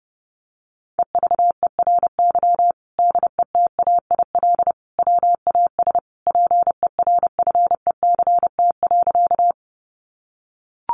Morse code, heard as a keyed tone, about 30 wpm, 700 Hz. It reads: EVERY DETAIL WAS PERFECT.
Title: Sentences from Top 1500 Words 30wpm